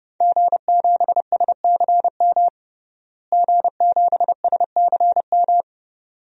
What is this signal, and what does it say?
Morse code, heard as a keyed tone, about 30 wpm, 700 Hz. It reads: G7SCM G7SCM
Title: DXCC-100 Call Signs - 5 Characters - Repeated 2x - 30wpm